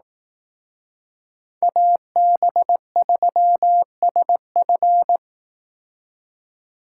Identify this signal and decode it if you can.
Morse code, heard as a keyed tone, about 18 wpm, 700 Hz. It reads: AB3SF
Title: Speed Racing - USA 2x2 Call Signs 15wpm